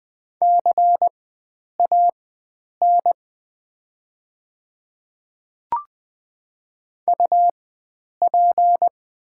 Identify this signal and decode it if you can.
Morse code, heard as a keyed tone, about 20 wpm, spaced as for 9 wpm, 690 Hz. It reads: CAN UP
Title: 4x Character Spacing - Top 100 Words - 20wpm